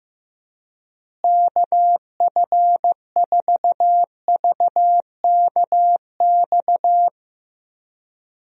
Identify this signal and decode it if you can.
Morse code, heard as a keyed tone, about 15 wpm, 700 Hz. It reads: KF4VKX
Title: USA Call Signs - 6 Characters - Repeated 2x - 15wpm